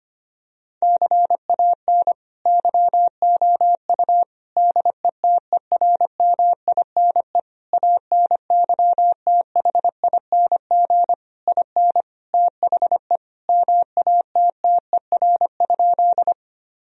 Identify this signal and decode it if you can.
Morse code, heard as a keyed tone, about 25 wpm, 700 Hz. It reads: CAN YOU DETERMINE ANYTHING IN THE MATTER?
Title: Sentences from Top 1000 Words - Review Part 4 - 25wpm